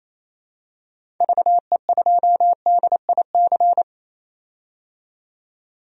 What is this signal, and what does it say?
Morse code, heard as a keyed tone, about 27 wpm, 700 Hz. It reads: VE2DIC
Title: DXCC-100 Call Signs - 6 Characters - 28wpm